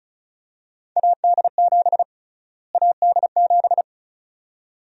Morse code, heard as a keyed tone, about 36 wpm, 700 Hz. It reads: AD7 AD7
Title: USA Call Sign Prefixes - Repeated 2x - 35wpm